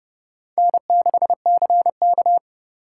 Morse code, heard as a keyed tone, about 28 wpm, 700 Hz. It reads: N6CK